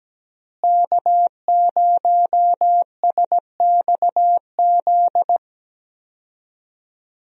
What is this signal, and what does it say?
Morse code, heard as a keyed tone, about 17 wpm, 700 Hz. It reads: K0SXZ
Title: USA Call Signs - 5 Characters - Repeated 3x - 17wpm